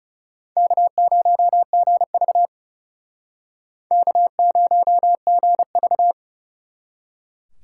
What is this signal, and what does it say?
Morse code, heard as a keyed tone, about 34 wpm, 700 Hz. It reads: K0GV K0GV
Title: Speed Racing - USA Call Signs 30wpm